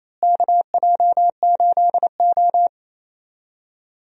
Morse code, heard as a keyed tone, about 28 wpm, 700 Hz. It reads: KJ8O
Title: USA Call Signs - 4 Characters - Repeated 2x - 28wpm